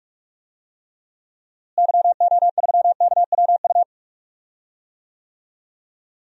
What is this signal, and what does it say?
Morse code, heard as a keyed tone, about 41 wpm, 700 Hz. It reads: YO3KWU
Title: DXCC-100 Call Signs - 6 Characters - Repeated 2x - 45wpm